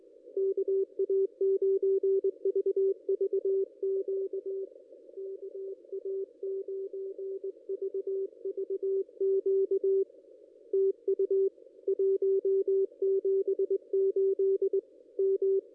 Morse code, heard as a keyed tone, about 20 wpm, 380 Hz, with radio noise. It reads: KA9VVQ KA9VVQ TU 178 M
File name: contests / ss-cw-k5tr-op-n5rz -0400